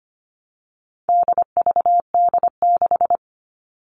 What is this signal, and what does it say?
Morse code, heard as a keyed tone, about 25 wpm, 700 Hz. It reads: DVD6
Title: All Cut Numbers - 4-Digit 25wpm